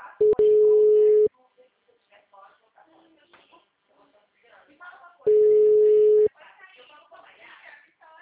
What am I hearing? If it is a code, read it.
Ringback tone (Europe, 425 Hz), with speech.